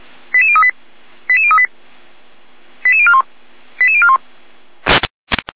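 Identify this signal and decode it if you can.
Five-tone selective call (ZVEI), heard as quick runs of five tones, tones of about 70 ms. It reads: 88038 88038 88041 88041
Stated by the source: Five-tone ZVEI selective call received on SDR.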